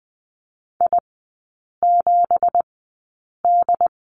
Morse code, heard as a keyed tone, about 20 wpm, spaced as for 8 wpm, 690 Hz. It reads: I7D
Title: 212 Lesson 34e - Number 8 - Three Characters 20wpm